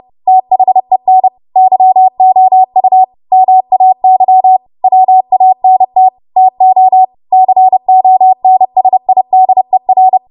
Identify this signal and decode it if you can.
Morse code, heard as a keyed tone, about 30 wpm, 750 Hz. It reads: THEN YOU MAY WANT TO CONSIDER
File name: Morse / 30WPM